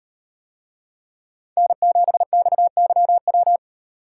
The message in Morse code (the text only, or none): N7XYW